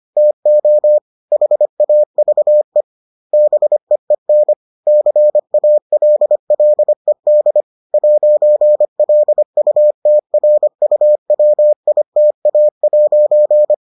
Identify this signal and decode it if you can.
Morse code, heard as a keyed tone, about 25 wpm, 600 Hz. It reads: TO HAVE BEEN CALLED 'LUTRUWITA'